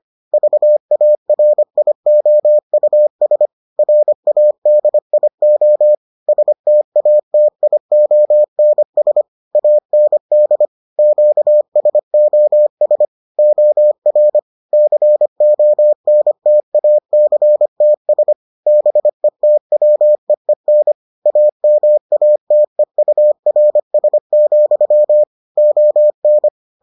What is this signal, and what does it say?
Morse code, heard as a keyed tone, about 25 wpm, 600 Hz. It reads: VARIOUS RADIO STATIONS AND QSOS OR CONTACTS BETWEEN AMATEURS, ON